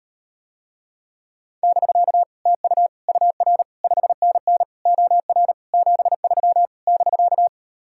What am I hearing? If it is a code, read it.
Morse code, heard as a keyed tone, about 39 wpm, 700 Hz. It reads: <BK> TU UR 5NN OR 73 <BK>